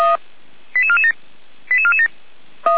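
Five-tone selective call (ZVEI), heard as quick runs of five tones, tones of about 70 ms. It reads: 88447 88447
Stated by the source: ZVEI five-tone selective call received on an SDR.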